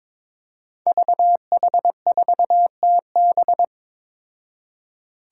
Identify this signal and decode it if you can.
Morse code, heard as a keyed tone, about 21 wpm, 700 Hz. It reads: VH4TB
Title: Five Challenging Letter-Number Combinations 22wpm